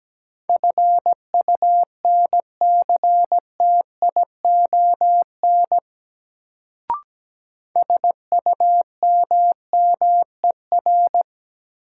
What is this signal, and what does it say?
Morse code, heard as a keyed tone, about 17 wpm, 700 Hz. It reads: FUNCTION SUMMER
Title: Sentences from Top 1100 Words - New Word Drill Alt - 17wpm